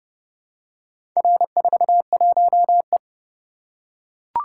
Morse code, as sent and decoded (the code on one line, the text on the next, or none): .-. ....- .---- .
R41E